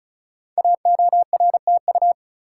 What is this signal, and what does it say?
Morse code, heard as a keyed tone, about 36 wpm, 690 Hz. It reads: AORTU
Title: Code Groups - 5 Letters - 35wpm